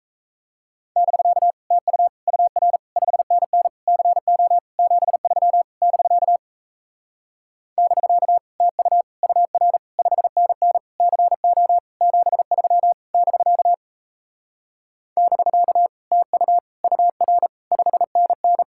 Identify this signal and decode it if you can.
Morse code, heard as a keyed tone, about 37 wpm, 700 Hz. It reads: <BK> TU UR 5NN CO 73 <BK> <BK> TU UR 5NN CO 73 <BK> <BK> TU UR 5NN